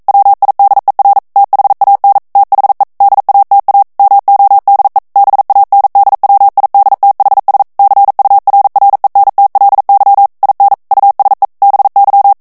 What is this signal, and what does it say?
Morse code, heard as a keyed tone, about 42 wpm, 800 Hz. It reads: WIDER THAN THE DATA MODE BANDWIDTHS CURRENTLY IN USE BY